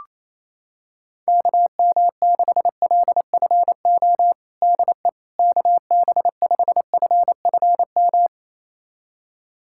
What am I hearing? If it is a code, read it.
Morse code, heard as a keyed tone, about 28 wpm, 700 Hz. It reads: KM6LFO DE KB5FFM
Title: USA Callsign DE Callsign - 6 Characters - 28wpm